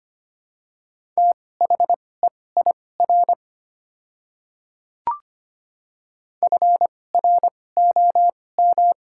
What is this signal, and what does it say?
Morse code, heard as a keyed tone, about 25 wpm, spaced as for 18 wpm, 700 Hz. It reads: THEIR FROM
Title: 2x Character Spacing - Top 200 Words - 25wpm